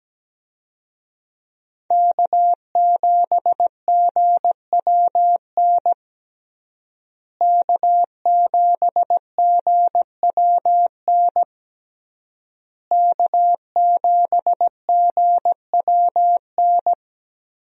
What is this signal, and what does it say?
Morse code, heard as a keyed tone, about 17 wpm, 700 Hz. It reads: K7GWN K7GWN K7GWN
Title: USA Call Signs - 5 Characters - Repeated 3x - 17wpm